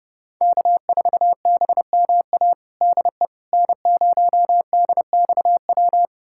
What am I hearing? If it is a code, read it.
Morse code, heard as a keyed tone, about 32 wpm, 700 Hz. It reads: K4BMA DE N0DXW
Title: USA Callsign DE Callsign - 5 Characters - 30wpm